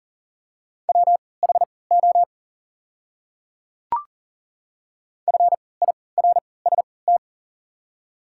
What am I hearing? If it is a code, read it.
Morse code, heard as a keyed tone, about 40 wpm, spaced as for 23 wpm, 690 Hz. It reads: WHO FIRST
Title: 3x Character Spacing - Top 100 Words - 40wpm